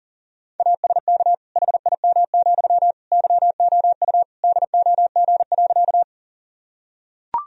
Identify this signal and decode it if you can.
Morse code, heard as a keyed tone, about 37 wpm, 700 Hz. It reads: ASK HIM, YOU DOG.
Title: Sentences from Top 1000 Words - Review Part 2 - 40wpm